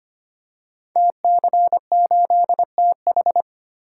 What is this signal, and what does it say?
Morse code, heard as a keyed tone, about 25 wpm, 700 Hz. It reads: TC8TH
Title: Code Groups - 5 Letter:Numbers - 25wpm